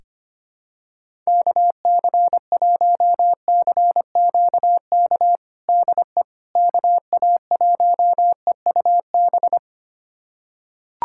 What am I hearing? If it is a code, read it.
Morse code, heard as a keyed tone, about 25 wpm, 700 Hz. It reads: KC1CQK DE KA1EUB
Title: USA Callsign DE Callsign 25wpm